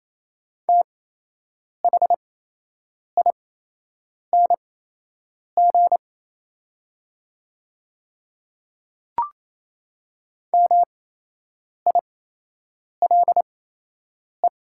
Morse code, heard as a keyed tone, about 28 wpm, spaced as for 8 wpm, 700 Hz. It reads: THING MILE
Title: 8x Character Spacing - Top 300 Words - 28wpm